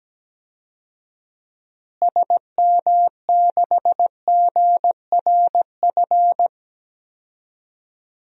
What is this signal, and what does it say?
Morse code, heard as a keyed tone, about 17 wpm, 700 Hz. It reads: SM6GRF
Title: DXCC-100 Call Signs - 6 Characters - Repeated 2x - 17wpm